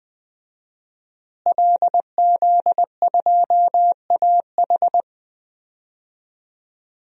Morse code, heard as a keyed tone, about 21 wpm, 700 Hz. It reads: LZ2AH
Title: DXCC-100 Call Signs - 5 Characters - Repeated 2x - 20wpm